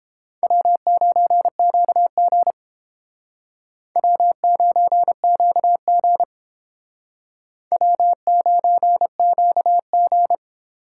Morse code, heard as a keyed tone, about 30 wpm, 700 Hz. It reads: W9QG W9QG W9QG